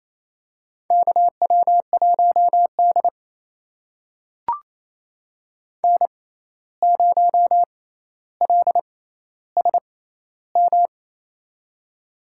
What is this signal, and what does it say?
Morse code, heard as a keyed tone, about 28 wpm, spaced as for 13 wpm, 700 Hz. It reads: KW1D N0LSM
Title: USA Call Signs - 6x Character Spacing - 28wpm